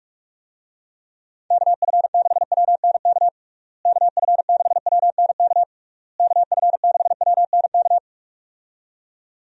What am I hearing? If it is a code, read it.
Morse code, heard as a keyed tone, about 46 wpm, 700 Hz. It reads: KF6WNK KF6WNK KF6WNK
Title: USA Call Signs - 6 Characters - Repeated 3x - 45wpm